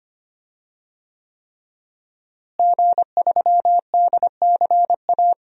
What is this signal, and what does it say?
Morse code, heard as a keyed tone, about 25 wpm, 700 Hz. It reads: G3DCA
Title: DXCC-100 Call Signs - 5 Characters - Repeated 3x - 25wpm